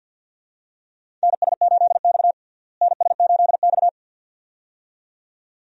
Morse code, heard as a keyed tone, about 52 wpm, 700 Hz. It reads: NS8X NS8X
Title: USA Call Signs - 4 Characters - Repeated 2x - 50wpm